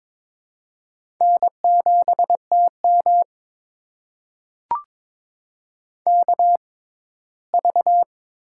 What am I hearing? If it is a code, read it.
Morse code, heard as a keyed tone, about 22 wpm, spaced as for 15 wpm, 700 Hz. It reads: N7TM KV
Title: USA Call Signs - 4 Characters - 6x Character Spacing - 22wpm